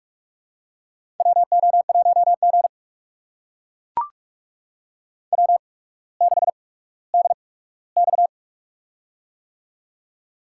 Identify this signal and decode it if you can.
Morse code, heard as a keyed tone, about 44 wpm, spaced as for 19 wpm, 700 Hz. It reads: WO1G W6DX
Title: USA Call Signs - 4 Characters - 8x Character Spacing - 45wpm